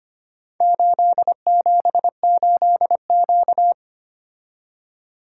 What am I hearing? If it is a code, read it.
Morse code, heard as a keyed tone, about 25 wpm, 700 Hz. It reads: 878Q
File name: practice / 25wpm / Four Letter-Number Combinations - 25wpm